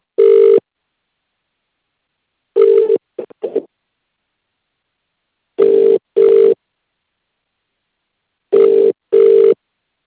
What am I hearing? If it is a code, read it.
Ringback tone (UK), 400 + 450 Hz.